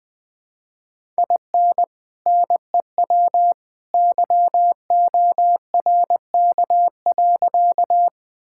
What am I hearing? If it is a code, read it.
Morse code, heard as a keyed tone, about 20 wpm, 700 Hz. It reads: IN NEW YORK.
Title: Sentences with 2-Word Phrases - Spoken Beginning - 20wpm